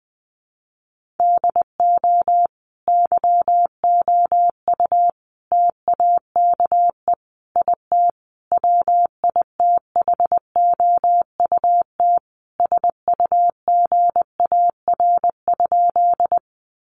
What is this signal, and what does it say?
Morse code, heard as a keyed tone, about 20 wpm, 700 Hz. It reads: DO YOU TAKE IT WITHOUT SUGAR?
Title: Sentences from Top 1200 Words 20wpm